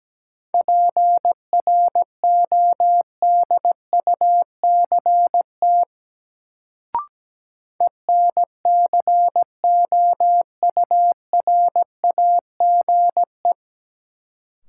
Morse code, heard as a keyed tone, about 17 wpm, 700 Hz. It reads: PRODUCT ENCOURAGE
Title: Sentences from Top 1400 Words - New Word Drill Alt - 17wpm